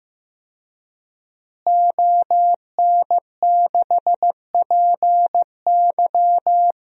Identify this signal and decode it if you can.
Morse code, heard as a keyed tone, about 15 wpm, 700 Hz. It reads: ON6PY
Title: DXCC-100 Call Signs - 5 Characters - Repeated 2x - 15wpm